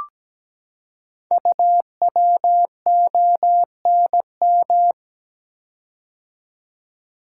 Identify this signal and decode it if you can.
Morse code, heard as a keyed tone, about 17 wpm, 700 Hz. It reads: UWONM
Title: Code Groups - 5 Letters - 17wpm